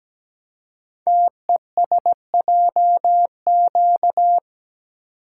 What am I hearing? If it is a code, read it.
Morse code, heard as a keyed tone, about 17 wpm, 700 Hz. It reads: TESJQ
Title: Code Groups - 5 Letters - 17wpm